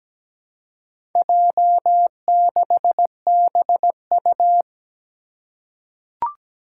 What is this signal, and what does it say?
Morse code, heard as a keyed tone, about 17 wpm, 700 Hz. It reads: J6BU
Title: Four Challenging Letter-Number Combinations 17wpm